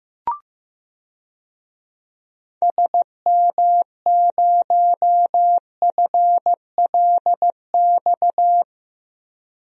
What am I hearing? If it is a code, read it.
Morse code, heard as a keyed tone, about 16 wpm, 700 Hz. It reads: SM0FLX
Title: DXCC-100 Call Signs - 6 Characters - Repeated 3x - 15wpm